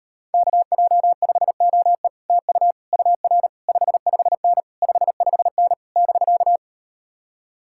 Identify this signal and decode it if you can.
Morse code, heard as a keyed tone, about 38 wpm, 700 Hz. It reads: KJ5OE TU UR 55N 55N <BK>